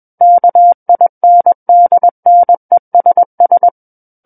Morse code, heard as a keyed tone, about 21 wpm, 700 Hz. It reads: KINDNESS
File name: Book One / 12-Chapter 12 - 21 WPM